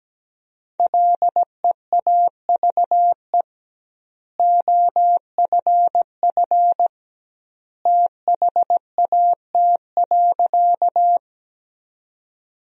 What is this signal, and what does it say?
Morse code, heard as a keyed tone, about 17 wpm, 700 Hz. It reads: LEAVE OFF THAT.
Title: 2x Word-Spacing - Sentences from Top 400 Words - 17wpm